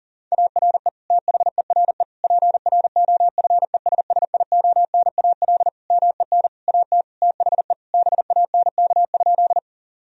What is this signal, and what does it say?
Morse code, heard as a keyed tone, about 40 wpm, 700 Hz. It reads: ARE THERE PROFESSIONAL MEN AT THE BANK?
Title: Sentences from Top 1000 Words - Review Part 4 - 40wpm